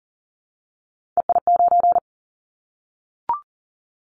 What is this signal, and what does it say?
Morse code, heard as a keyed tone, about 40 wpm, 700 Hz. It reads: EI9
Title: Callsign Prefixes for Top 50 DXCC - 40wpm